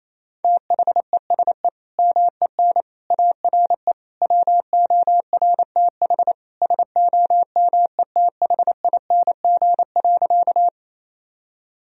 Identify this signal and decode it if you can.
Morse code, heard as a keyed tone, about 28 wpm, 700 Hz. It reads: THESE MEN ARE WORTH SOMETHING.
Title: Sentences from Top 1000 Words - Review Part 1 - 28wpm